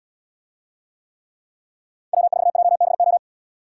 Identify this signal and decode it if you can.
Morse code, heard as a keyed tone, about 72 wpm, 700 Hz. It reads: W4JBO